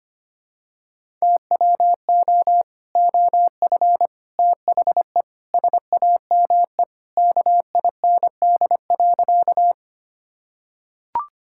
Morse code, heard as a keyed tone, about 25 wpm, 700 Hz. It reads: TWO OF THE SAME KIND.
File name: practice / 25wpm / Sentences with 3-Word Phrases - 25wpm